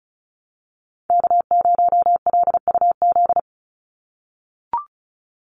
Morse code, heard as a keyed tone, about 34 wpm, 700 Hz. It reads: K0LUZ
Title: Call Signs - 35wpm